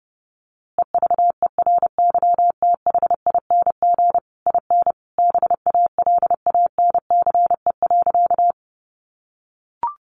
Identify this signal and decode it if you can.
Morse code, heard as a keyed tone, about 30 wpm, 700 Hz. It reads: EVERYTHING IN BALANCE.
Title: A Sound Of Thunder 30wpm